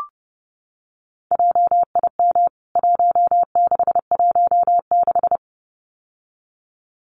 Morse code, heard as a keyed tone, about 30 wpm, 700 Hz. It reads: JIM 1616
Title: CWT Exchange 30wpm